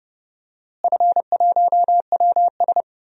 Morse code, heard as a keyed tone, about 30 wpm, 700 Hz. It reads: F1WS
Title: DXCC-100 Call Signs - 4 Characters - Repeated 2x - 30wpm